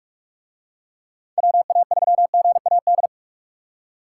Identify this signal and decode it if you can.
Morse code, heard as a keyed tone, about 46 wpm, 700 Hz. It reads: WA3GAD